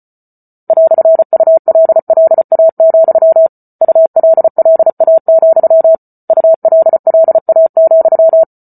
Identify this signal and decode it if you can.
Morse code, heard as a keyed tone, about 34 wpm, 650 Hz. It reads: "ULLA, ULLA, ULLA,